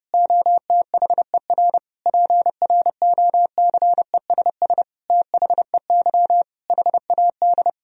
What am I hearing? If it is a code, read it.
Morse code, heard as a keyed tone, about 30 wpm, 700 Hz. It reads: OTHER PROCESS THEY HAD
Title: Sentences from Top 1000 Words - Review Part 3 - 30wpm